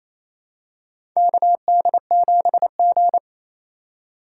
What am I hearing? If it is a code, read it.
Morse code, heard as a keyed tone, about 27 wpm, 700 Hz. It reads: KD7G
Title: USA Call Signs - 3x Character Spacing - 28wpm